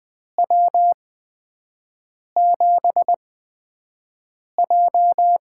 Morse code, heard as a keyed tone, about 20 wpm, spaced as for 5 wpm, 700 Hz. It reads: W7J
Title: USA Call Signs - 4 Characters - 8x Character Spacing - 20wpm